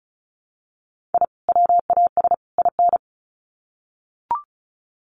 Morse code, heard as a keyed tone, about 34 wpm, 700 Hz. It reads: I WAS IN
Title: Sets of 3 Words 35wpm